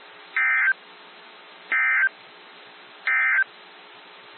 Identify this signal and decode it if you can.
Emergency Alert System (SAME) tones, heard as harsh buzzing bursts. It is heard from 0.2 to 3.5 s.
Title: NOAA Weather Radio - WXJ73